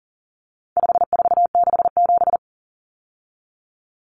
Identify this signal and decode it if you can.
Morse code, heard as a keyed tone, about 40 wpm, 700 Hz. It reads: 5467
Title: Common Cut Numbers - 4-Digit 40wpm